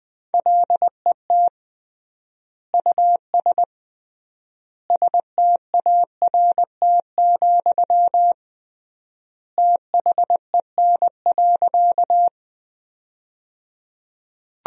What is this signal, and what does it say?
Morse code, heard as a keyed tone, about 20 wpm, 700 Hz. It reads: LET US START, THEN.
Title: 3x Word-Spacing - Sentences from Top 300 Words - 20wpm